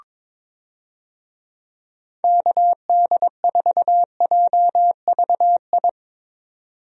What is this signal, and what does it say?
Morse code, heard as a keyed tone, about 22 wpm, 700 Hz. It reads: KD4JVI
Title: USA Call Signs - 6 Characters - 22wpm